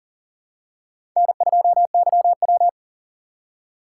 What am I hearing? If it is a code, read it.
Morse code, heard as a keyed tone, about 40 wpm, 700 Hz. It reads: N2YW
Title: USA Call Signs - 4 Characters - 4x Character Spacing - 40wpm